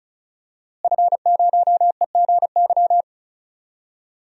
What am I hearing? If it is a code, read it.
Morse code, heard as a keyed tone, about 34 wpm, 700 Hz. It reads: F0EGY
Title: DXCC-100 Call Signs - 5 Characters - Repeated 2x - 35wpm